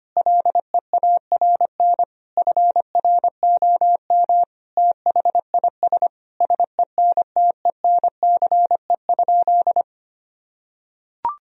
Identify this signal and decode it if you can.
Morse code, heard as a keyed tone, about 25 wpm, 700 Hz. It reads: LEARN FROM THIS SENTENCE?
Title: Sentences from Top 1000 Words - Review Part 1 - 25wpm